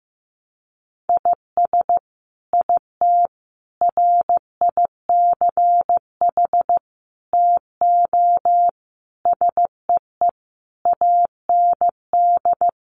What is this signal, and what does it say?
Morse code, heard as a keyed tone, about 15 wpm, 700 Hz. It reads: IS IT RICH TO SEE AND
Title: Sentences from Top 1100 Words 15wpm